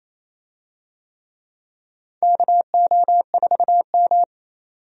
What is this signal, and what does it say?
Morse code, heard as a keyed tone, about 28 wpm, 700 Hz. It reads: KO4M
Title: USA Call Signs - 4 Characters - 28wpm